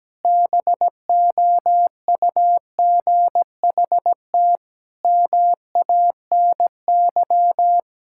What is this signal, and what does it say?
Morse code, heard as a keyed tone, about 17 wpm, 700 Hz. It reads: BOUGHT MANY
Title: Pangram Challenge v2 - 17wpm